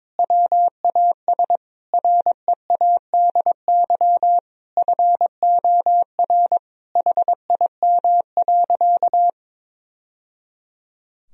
Morse code, heard as a keyed tone, about 21 wpm, 700 Hz. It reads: WAS READY FOR HIM.
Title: Sentences from Top 1000 Words - Review Part 3 - 22wpm